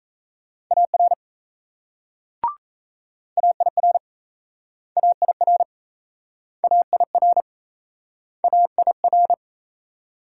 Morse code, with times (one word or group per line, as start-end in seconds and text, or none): 0.7-1.1 AR
3.4-4.0 AIR
5.0-5.6 AIR
6.6-7.4 AIR
8.4-9.3 AIR